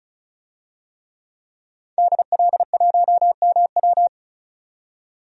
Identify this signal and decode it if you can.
Morse code, heard as a keyed tone, about 34 wpm, 700 Hz. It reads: DL1MW
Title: DXCC-100 Call Signs - 5 Characters - 35wpm